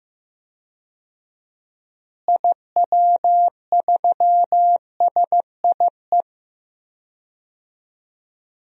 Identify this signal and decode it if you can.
Morse code, heard as a keyed tone, about 15 wpm, 700 Hz. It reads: IW3SIE
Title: DXCC-100 Call Signs - 6 Characters - Repeated 3x - 15wpm